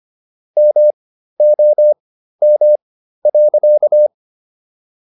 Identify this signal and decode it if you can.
Morse code, heard as a keyed tone, about 25 wpm, spaced as for 13 wpm, 600 Hz. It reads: MOM.